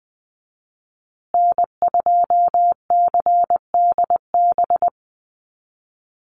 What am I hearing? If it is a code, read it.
Morse code, heard as a keyed tone, about 20 wpm, 700 Hz. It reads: N2CDB
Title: USA Call Signs Repeated 2x 20wpm